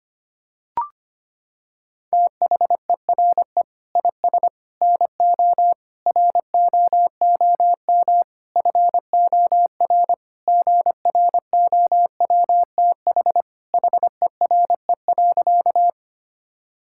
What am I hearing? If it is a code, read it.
Morse code, heard as a keyed tone, about 25 wpm, 700 Hz. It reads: THERE IS NO ROOM FOR GROWTH HERE.
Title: Sentences from Top 1000 Words - Review Part 5 - 25wpm